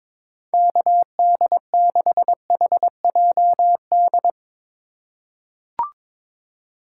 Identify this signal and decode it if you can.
Morse code, heard as a keyed tone, about 22 wpm, 700 Hz. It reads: KD6HJD
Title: USA Call Signs - 6 Characters - 3x Character Spacing - 22wpm